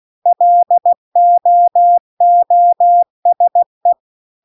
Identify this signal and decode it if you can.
Morse code, heard as a keyed tone, about 16 wpm, 700 Hz. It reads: LOOSE